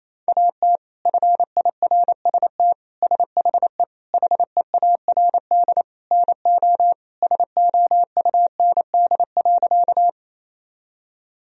Morse code, heard as a keyed tone, about 28 wpm, 700 Hz. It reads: AT FIRST SHE HEARD NO SOUND.
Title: Sentences from Top 1000 Words - Review Part 1 - 28wpm